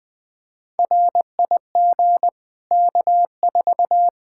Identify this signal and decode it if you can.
Morse code, heard as a keyed tone, about 20 wpm, 700 Hz. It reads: RIG K4
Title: Practice QSO - Exchange 2 - Info - 20wpm